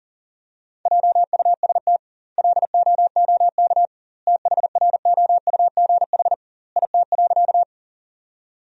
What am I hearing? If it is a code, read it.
Morse code, heard as a keyed tone, about 37 wpm, 700 Hz. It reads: JUST LOOK THROUGH IT.
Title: 2x Word-Spacing - Sentences from Top 200 Words - 40wpm